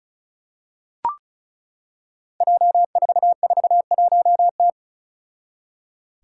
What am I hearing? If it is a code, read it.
Morse code, heard as a keyed tone, about 36 wpm, 700 Hz. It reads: J441T
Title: Five Challenging Letter-Number Combinations 35wpm